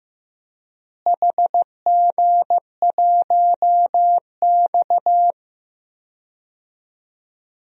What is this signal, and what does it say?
Morse code, heard as a keyed tone, about 15 wpm, 700 Hz. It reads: HG1X
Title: DXCC-100 Call Signs - 4 Characters - Repeated 2x - 15wpm